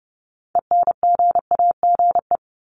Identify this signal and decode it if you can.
Morse code, heard as a keyed tone, about 30 wpm, 700 Hz. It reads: ENGAGE
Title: Top 1251-1500 Words 30wpm